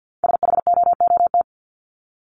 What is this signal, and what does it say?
Morse code, heard as a keyed tone, about 48 wpm, 700 Hz. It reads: SHOOT
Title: Top 1001-1250 Words 50wpm